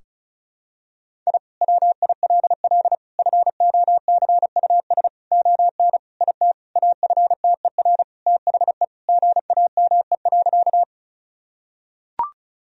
Morse code, heard as a keyed tone, about 36 wpm, 700 Hz. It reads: I WILL FOCUS ON IT AFTER THE GAME.
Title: Sentences from Top 1000 Words - Review Part 4 - 35wpm